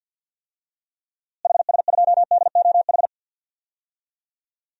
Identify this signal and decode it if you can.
Morse code, heard as a keyed tone, about 51 wpm, 700 Hz. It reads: HS2DOH
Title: DXCC-100 Call Signs - 6 Characters - Repeated 2x - 50wpm